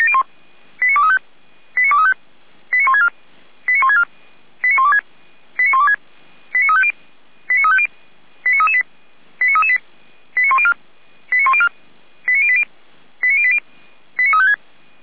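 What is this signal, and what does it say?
Five-tone selective call (ZVEI), heard as quick runs of five tones, tones of about 70 ms. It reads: 89235 89235 89164 89164 89126 89126 89350 89350 89338 89338 89104 89104 89080 89080 89356
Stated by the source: ZVEI five-tone call 89700 received on an SDR.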